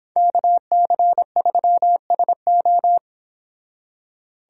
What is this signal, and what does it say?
Morse code, heard as a keyed tone, about 24 wpm, 700 Hz. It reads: KC3SO
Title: Speed Racing - USA 2x2 Call Signs 22wpm